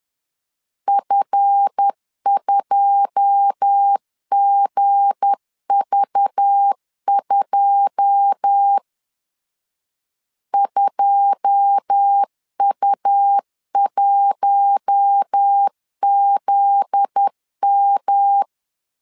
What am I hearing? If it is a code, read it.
Morse code, heard as a keyed tone, about 10 wpm, 800 Hz. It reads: F2GV2 2U1ZM
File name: LEZIONI / 9 2 10wpm 2736